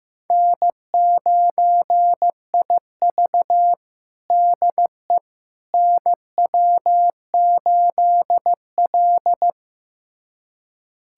Morse code, heard as a keyed tone, about 15 wpm, 700 Hz. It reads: N9IV DE NW8L
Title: USA Callsign DE Callsign - 4 Characters - 15wpm